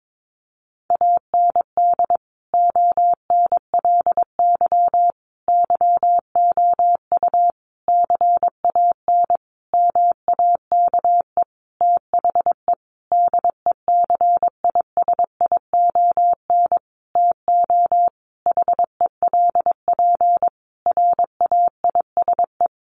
Morse code, heard as a keyed tone, about 22 wpm, 700 Hz. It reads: AND ONLY YOU CAN MAKE THE DECISION TO HELP RAISE
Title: Sentences from Top 1400 Words 22wpm